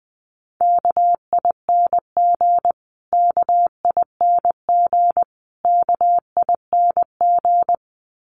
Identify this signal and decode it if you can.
Morse code, heard as a keyed tone, about 20 wpm, 700 Hz. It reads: KING KING KING
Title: Top 500 Words Repeated 20wpm